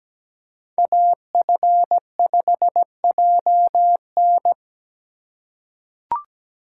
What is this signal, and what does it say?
Morse code, heard as a keyed tone, about 17 wpm, 700 Hz. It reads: AF5JN
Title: USA Call Signs - 5 Characters - 2x Character Spacing - 17wpm